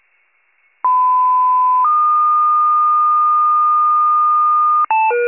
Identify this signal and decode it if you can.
Two-tone page, heard as a one-second tone, then a three-second tone, about 1010 then 1220 Hz. It is heard from 0.8 to 4.9 s.